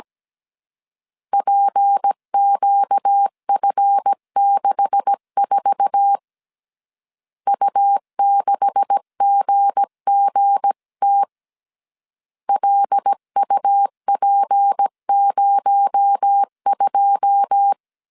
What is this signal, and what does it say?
Morse code, heard as a keyed tone, about 17 wpm, 800 Hz. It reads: PQF64 U6GGT LUP02
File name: CW / DET / 1 9 1500C S A 16wpm 2828